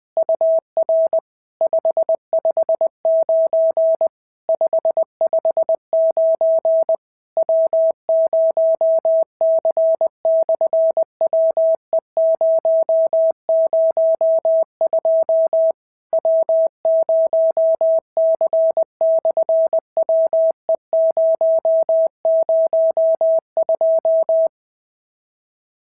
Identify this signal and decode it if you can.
Morse code, heard as a keyed tone, about 20 wpm, 650 Hz. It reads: UR 559 559 W0C/WE002 W0C/WE002
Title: SOTA QSO Practice 20wpm